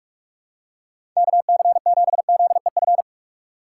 Morse code, heard as a keyed tone, about 47 wpm, 700 Hz. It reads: KC7ZEF
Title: USA Call Signs - 6 Characters - 4x Character Spacing - 45wpm